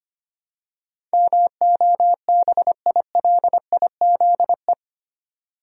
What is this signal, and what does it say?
Morse code, heard as a keyed tone, about 25 wpm, 700 Hz. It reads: MOBILIZE
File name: 8 Letter Words - First 3 Letters Given - 25wpm